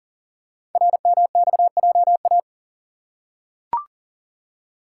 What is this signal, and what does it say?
Morse code, heard as a keyed tone, about 39 wpm, 700 Hz. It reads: RMXJA
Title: Code Groups - 5 Letters - 40wpm